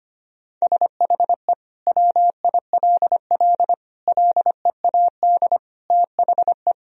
Morse code, heard as a keyed tone, about 25 wpm, 700 Hz. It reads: SHE WILL LEAD THE
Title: Sentences from Top 1000 Words - Review Part 2 - 25wpm